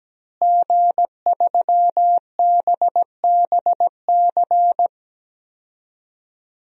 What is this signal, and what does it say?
Morse code, heard as a keyed tone, about 17 wpm, 700 Hz. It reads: G3BBC
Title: DXCC-100 Call Signs - 5 Characters - Repeated 3x - 17wpm